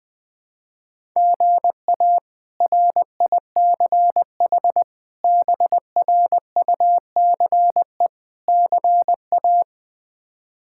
Morse code, heard as a keyed tone, about 20 wpm, 700 Hz. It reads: GA RICH BRUCE CA
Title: K1USN SST QSO Practice - 20wpm